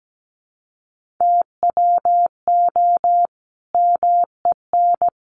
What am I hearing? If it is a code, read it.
Morse code, heard as a keyed tone, about 17 wpm, 700 Hz. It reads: TWO MEN
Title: Sentences from Top 600 Words 17wpm